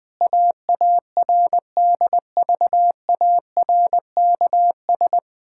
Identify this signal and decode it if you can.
Morse code, heard as a keyed tone, about 19 wpm, 700 Hz. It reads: AARDVARKS
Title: Pangram Challenge v2 - 20wpm